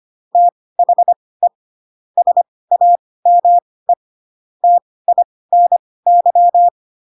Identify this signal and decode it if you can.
Morse code, heard as a keyed tone, about 25 wpm, spaced as for 18 wpm, 700 Hz. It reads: THE SAME TINY
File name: Pinocchio / Pinocchio ch1 12 25 ik1hge